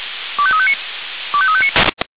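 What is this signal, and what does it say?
Five-tone selective call (ZVEI), heard as quick runs of five tones, tones of about 70 ms. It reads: 25359 25359
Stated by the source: ZVEI five-tone selective call received on SDR.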